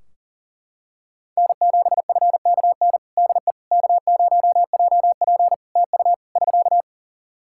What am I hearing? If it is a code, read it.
Morse code, heard as a keyed tone, about 40 wpm, 700 Hz. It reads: N7FKN DE K0JP TU <SK>
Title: Practice QSO - Exchange 4 - Ending - 40wpm